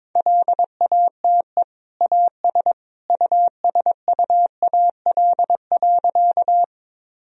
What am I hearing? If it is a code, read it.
Morse code, heard as a keyed tone, about 22 wpm, 700 Hz. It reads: LATE AS USUAL.
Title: Sentences from Top 1000 Words - Review Part 2 - 22wpm